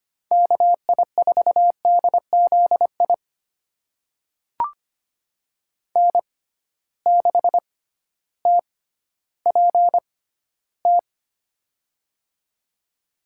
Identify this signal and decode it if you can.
Morse code, heard as a keyed tone, about 25 wpm, spaced as for 13 wpm, 700 Hz. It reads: KI4DZI N6TPT
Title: USA Call Signs - 6x Character Spacing - 25wpm